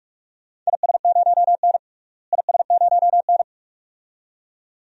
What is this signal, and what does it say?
Morse code, heard as a keyed tone, about 46 wpm, 700 Hz. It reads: IS0N IS0N
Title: DXCC-100 Call Signs - 4 Characters - Repeated 2x - 45wpm